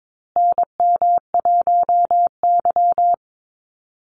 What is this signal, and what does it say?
Morse code, heard as a keyed tone, about 22 wpm, 700 Hz. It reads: NM1Y
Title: Call Signs Repeated 2x - 22wpm